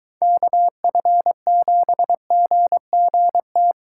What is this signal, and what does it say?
Morse code, heard as a keyed tone, about 23 wpm, 700 Hz. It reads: KF7GGT